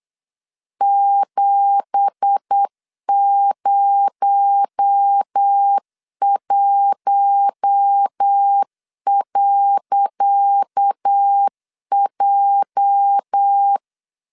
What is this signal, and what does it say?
Morse code, heard as a keyed tone, about 8 wpm, 800 Hz. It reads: 701.J
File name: LEZIONI / 10 SIGNS / 10C DOT 8wpm 3649